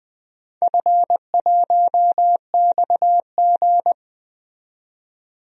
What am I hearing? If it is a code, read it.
Morse code, heard as a keyed tone, about 20 wpm, 700 Hz. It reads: F1XG
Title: DXCC-100 Call Signs - 4 Characters - 20wpm